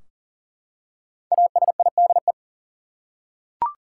Morse code, heard as a keyed tone, about 40 wpm, 700 Hz. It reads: ASIDE
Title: Five Letter Words - 40wpm